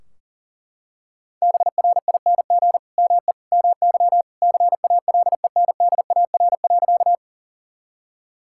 Morse code, heard as a keyed tone, about 40 wpm, 700 Hz. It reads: BRING ME MY CALENDAR.
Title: Sentences from Top 1600 Words 40wpm